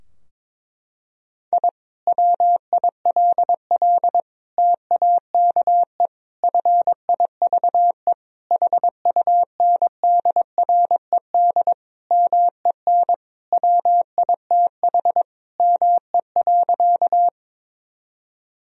Morse code, heard as a keyed tone, about 21 wpm, 700 Hz. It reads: I WILL TAKE FIVE HUNDRED MEN WITH ME.
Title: Sentences from Top 1000 Words - Review Part 2 - 22wpm